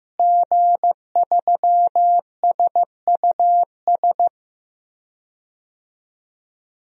Morse code, heard as a keyed tone, about 15 wpm, 700 Hz. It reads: G3SUS
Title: DXCC-100 Call Signs - 5 Characters - Repeated 2x - 15wpm